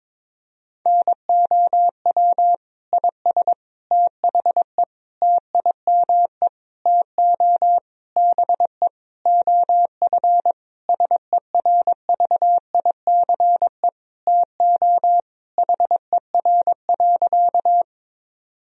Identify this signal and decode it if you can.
Morse code, heard as a keyed tone, about 21 wpm, 700 Hz. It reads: NOW IS THE TIME TO BE OF SERVICE TO HER.
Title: Sentences from Top 1000 Words - Review Part 4 - 22wpm